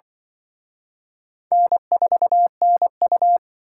Morse code, heard as a keyed tone, about 23 wpm, 700 Hz. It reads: N4NU
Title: Speed Racing - USA 1x2 Call Signs 20wpm